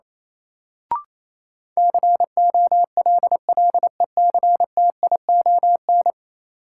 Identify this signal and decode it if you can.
Morse code, heard as a keyed tone, about 28 wpm, 700 Hz. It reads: COLLECTION
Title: Sentences from Top 1200 Words - New Word Drill - 28wpm